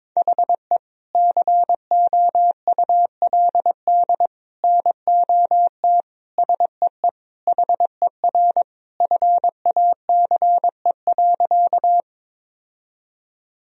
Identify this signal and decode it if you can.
Morse code, heard as a keyed tone, about 22 wpm, 700 Hz. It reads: HE COULD NOT SEE HER FACE.
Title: Sentences from Top 1000 Words - Review Part 2 - 22wpm